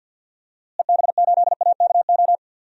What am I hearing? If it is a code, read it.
Morse code, heard as a keyed tone, about 51 wpm, 700 Hz. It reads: EB8AKO
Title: DXCC-100 Call Signs - 6 Characters - Repeated 3x - 50wpm